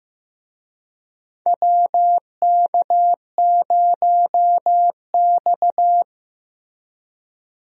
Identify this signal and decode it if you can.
Morse code, heard as a keyed tone, about 15 wpm, 700 Hz. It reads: WK0X
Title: USA Call Signs - 4 Characters - Repeated 2x - 15wpm